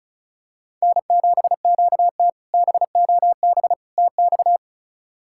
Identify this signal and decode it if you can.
Morse code, heard as a keyed tone, about 36 wpm, 700 Hz. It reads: N7QT BOB TX
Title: K1USN SST QSO Practice - 35wpm